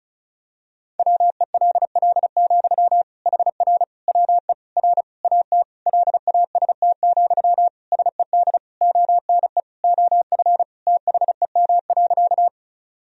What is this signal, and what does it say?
Morse code, heard as a keyed tone, about 36 wpm, 700 Hz. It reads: WELL, HR WE R AT LAST, SED ONE OF THEM.
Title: Sentences from Top 300 Words and Abbreviations 35wpm